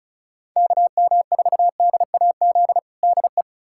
Morse code, heard as a keyed tone, about 36 wpm, 700 Hz. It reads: KM4DAZ DE
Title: USA Callsign DE Callsign 35wpm